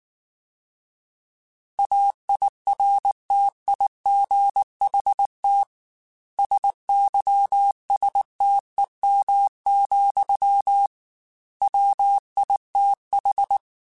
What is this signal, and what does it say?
Morse code, heard as a keyed tone, about 19 wpm, 800 Hz. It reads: AIRTIGHT SYSTEM, WITH